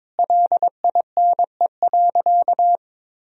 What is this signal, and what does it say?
Morse code, heard as a keyed tone, about 22 wpm, 700 Hz. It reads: LINE.
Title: Sentences from Top 1000 Words - Review Part 2 - 22wpm